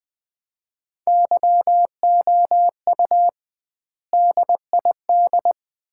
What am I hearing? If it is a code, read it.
Morse code, heard as a keyed tone, about 21 wpm, 700 Hz. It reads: YOU DID
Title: 2x Word-Spacing - Sets of 3 Words - 20wpm